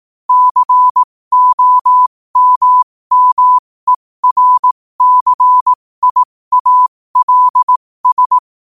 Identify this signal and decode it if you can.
Morse code, heard as a keyed tone, about 18 wpm, spaced as for 15 wpm, 1000 Hz. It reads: COMMERCIALS
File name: 15wpm